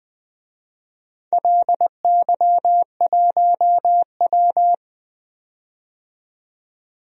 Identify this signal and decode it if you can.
Morse code, heard as a keyed tone, about 20 wpm, 700 Hz. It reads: LY1W
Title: DXCC-100 Call Signs - 4 Characters - 20wpm